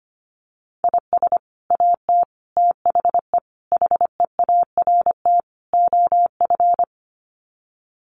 Morse code, heard as a keyed tone, about 25 wpm, 700 Hz. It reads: IS AT THE HEART OF
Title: Sets of 5 Words Encore2 25wpm